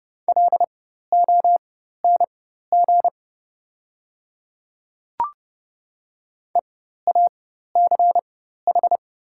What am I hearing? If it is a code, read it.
Morse code, heard as a keyed tone, about 30 wpm, spaced as for 14 wpm, 690 Hz. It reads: LONG EACH
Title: 4x Character Spacing - Top 100 Words - 30wpm